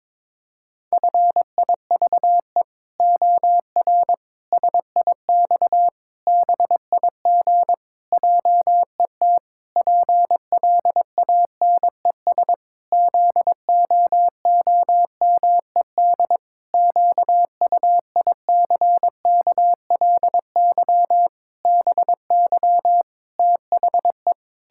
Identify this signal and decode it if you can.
Morse code, heard as a keyed tone, about 22 wpm, 700 Hz. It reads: FIVE OR SIX BIG JET PLANES ZOOMED QUICKLY BY THE
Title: Pangram Challenge v2 - 22wpm